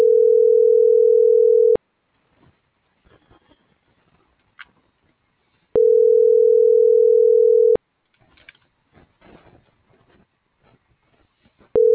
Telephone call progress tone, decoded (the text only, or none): ringback tone